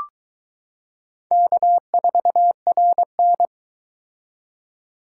Morse code, heard as a keyed tone, about 23 wpm, 700 Hz. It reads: K4RN